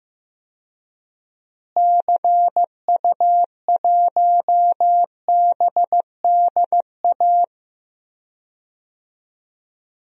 Morse code, heard as a keyed tone, about 15 wpm, 700 Hz. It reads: CU1BDA
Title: DXCC-100 Call Signs - 6 Characters - Repeated 2x - 15wpm